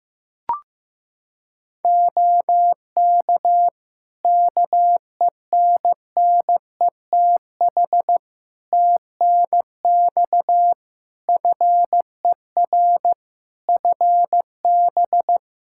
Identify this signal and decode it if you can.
Morse code, heard as a keyed tone, about 15 wpm, 700 Hz. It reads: OK KENNETH TNX FER FB
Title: Practice QSO - Exchange 4 - Ending - 15wpm